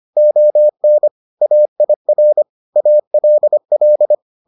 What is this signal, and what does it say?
Morse code, heard as a keyed tone, about 25 wpm, 600 Hz. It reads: ON AIR ALL